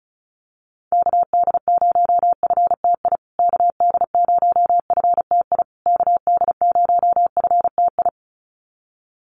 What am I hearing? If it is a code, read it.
Morse code, heard as a keyed tone, about 34 wpm, 700 Hz. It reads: KD0FTI KD0FTI KD0FTI
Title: USA Call Signs Repeated 3x 35wpm